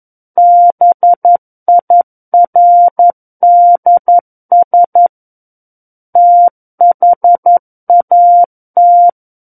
Morse code, heard as a keyed tone, about 11 wpm, 700 Hz. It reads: BIRDS THAT
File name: Book One / 02-Chapter 2 - 11 WPM